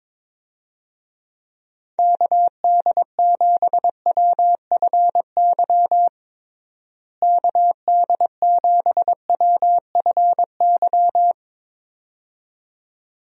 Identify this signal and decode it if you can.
Morse code, heard as a keyed tone, about 22 wpm, 700 Hz. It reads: KD7WFY KD7WFY
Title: USA Call Signs - 6 Characters - Repeated 2x - 22wpm